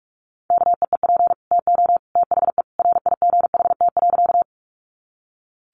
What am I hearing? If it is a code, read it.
Morse code, heard as a keyed tone, about 44 wpm, 700 Hz. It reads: KEEP TO THE RIGHT.
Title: Sentences from Top 300 Words 45wpm